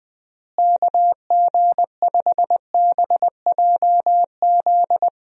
Morse code, heard as a keyed tone, about 20 wpm, 700 Hz. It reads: KG5BJZ
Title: USA Call Signs - 6 Characters - Repeated 2x - 20wpm